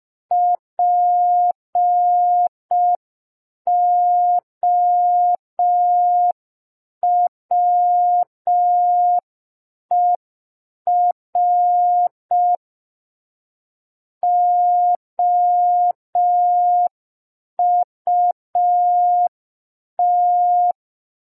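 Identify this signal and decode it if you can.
Morse code, heard as a keyed tone, about 5 wpm, 710 Hz. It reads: POWER OUT
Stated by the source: Morse receiving practice lesson for an amateur radio Foundation assessment.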